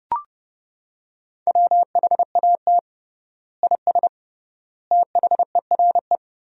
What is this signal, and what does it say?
Morse code, heard as a keyed tone, about 30 wpm, 700 Hz. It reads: WHAT IS THERE
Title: 3x Word-Spacing - Sentences from Top 400 Words - 30wpm